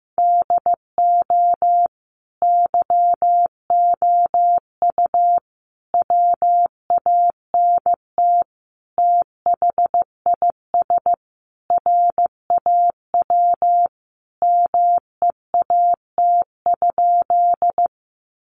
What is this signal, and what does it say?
Morse code, heard as a keyed tone, about 15 wpm, 700 Hz. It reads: DO YOU WANT THIS RAW MEAT?
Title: Sentences from Top 1500 Words 15wpm